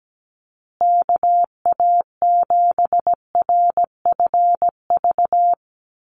Morse code, heard as a keyed tone, about 17 wpm, 700 Hz. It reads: KA7RFV
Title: USA Call Signs Repeated 2x 17wpm